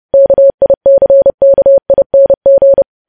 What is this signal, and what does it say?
Morse code, heard as a keyed tone, about 30 wpm, 550 Hz. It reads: KICKING
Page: news headlines in morse code at 30 wpm